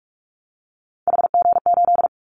Morse code, heard as a keyed tone, about 46 wpm, 700 Hz. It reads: HG8
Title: Callsign Prefixes for Top 50 DXCC - 45wpm